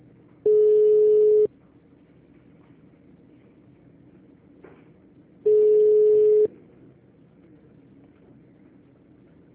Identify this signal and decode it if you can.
Ringback tone (Europe, 425 Hz).